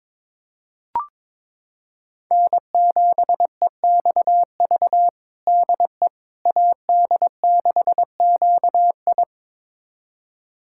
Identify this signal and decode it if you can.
Morse code, heard as a keyed tone, about 22 wpm, 700 Hz. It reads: N7EXV DE AD6QI
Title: USA Callsign DE Callsign - 5 Characters - 22wpm